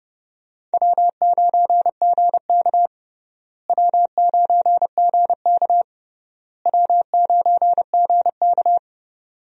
Morse code, heard as a keyed tone, about 30 wpm, 700 Hz. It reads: W9GK W9GK W9GK
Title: USA Call Signs - 4 Characters - Repeated 3x - 30wpm